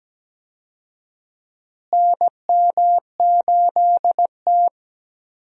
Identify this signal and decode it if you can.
Morse code, heard as a keyed tone, about 17 wpm, 700 Hz. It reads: NM8T
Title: USA Call Signs - 4 Characters - Repeated 3x - 17wpm